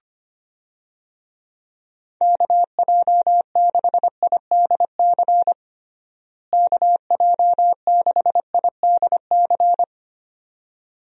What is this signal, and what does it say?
Morse code, heard as a keyed tone, about 25 wpm, 700 Hz. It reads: KJ6IDC KJ6IDC
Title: USA Call Signs - 6 Characters - Repeated 2x - 25wpm